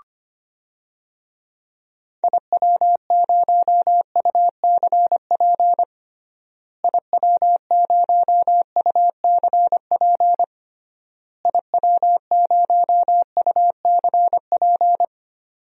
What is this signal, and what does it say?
Morse code, heard as a keyed tone, about 25 wpm, 700 Hz. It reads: IW0UCP IW0UCP IW0UCP
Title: DXCC-100 Call Signs - 6 Characters - Repeated 3x - 25wpm